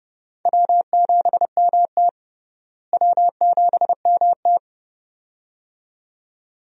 Morse code, heard as a keyed tone, about 30 wpm, 700 Hz. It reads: W7MT W7MT
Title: USA Call Signs - 4 Characters - Repeated 2x - 30wpm